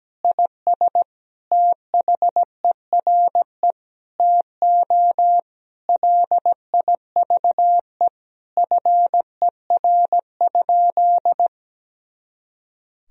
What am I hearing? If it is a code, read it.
Morse code, heard as a keyed tone, about 17 wpm, 700 Hz. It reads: IS THERE TO LIVE FER?
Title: Sentences from Top 200 Words and Abbreviations 17wpm